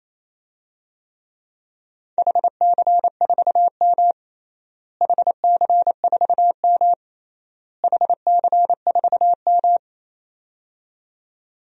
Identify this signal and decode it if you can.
Morse code, heard as a keyed tone, about 27 wpm, 700 Hz. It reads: HC4M HC4M HC4M
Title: DXCC-100 Call Signs - 4 Characters - Repeated 3x - 28wpm